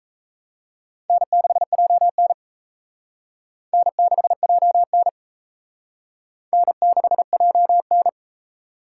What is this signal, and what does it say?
Morse code, heard as a keyed tone, about 34 wpm, 700 Hz. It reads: N6JN N6JN N6JN
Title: Speed Racing - USA Call Signs 28wpm